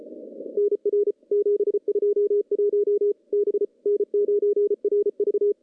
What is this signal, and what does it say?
Morse code, heard as a keyed tone, about 33 wpm, 410 Hz. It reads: NR 721 B N9RV
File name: CW-SS-k5tr